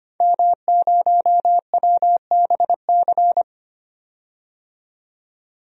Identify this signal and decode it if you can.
Morse code, heard as a keyed tone, about 25 wpm, 700 Hz. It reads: M0WBC
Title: DXCC-100 Call Signs - 5 Characters - 25wpm